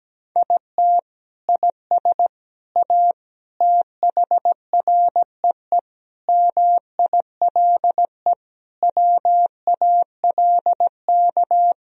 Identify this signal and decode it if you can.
Morse code, heard as a keyed tone, about 17 wpm, 700 Hz. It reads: IT IS A THREE MILE WALK
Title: Sentences from Top 1000 Words - Review Part 1 - 17wpm